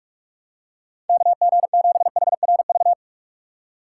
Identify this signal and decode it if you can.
Morse code, heard as a keyed tone, about 46 wpm, 700 Hz. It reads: KG7HRV
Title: USA Call Signs - 6 Characters - 2x Character Spacing - 45wpm